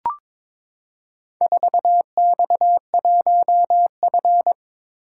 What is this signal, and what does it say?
Morse code, heard as a keyed tone, about 22 wpm, 700 Hz. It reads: 4X1F
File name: practice / 22wpm / Four Letter-Number Combinations - 22wpm